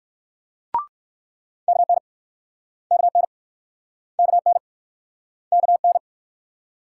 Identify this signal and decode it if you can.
Morse code, heard as a keyed tone, about 62 wpm, 700 Hz. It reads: KN KN KN KN